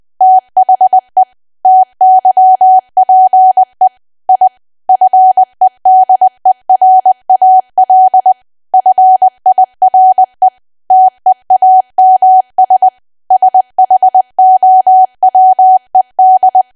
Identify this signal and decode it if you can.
Morse code, heard as a keyed tone, about 20 wpm, 750 Hz. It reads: THE TYPE I FEDERAL FIRE TEAMS SHOWED
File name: Archive / 20 WPM / 20WPM